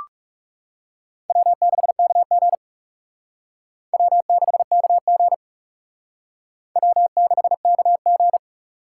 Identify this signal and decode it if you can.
Morse code, heard as a keyed tone, about 36 wpm, 700 Hz. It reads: W6KG W6KG W6KG
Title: Speed Racing - USA 1x2 Call Signs 30wpm